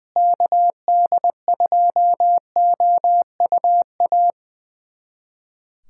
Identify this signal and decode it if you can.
Morse code, heard as a keyed tone, about 20 wpm, 700 Hz. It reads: KD2OUA
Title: USA Callsign DE Callsign 20wpm